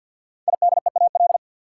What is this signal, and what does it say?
Morse code, heard as a keyed tone, about 53 wpm, 700 Hz. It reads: IDEAL
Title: Sentences from Top 1400 Words - New Word Drill - 50wpm